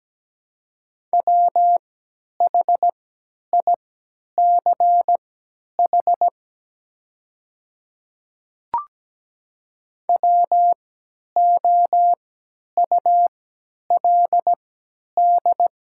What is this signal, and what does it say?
Morse code, heard as a keyed tone, about 17 wpm, spaced as for 10 wpm, 700 Hz. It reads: WHICH WOULD
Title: 3x Character Spacing - Top 100 Words - 17wpm